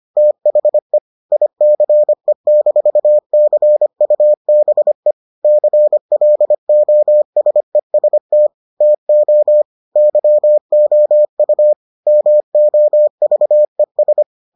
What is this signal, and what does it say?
Morse code, heard as a keyed tone, about 25 wpm, 600 Hz. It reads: THE ICE-CUBE CLOSEST TO YOU MOVES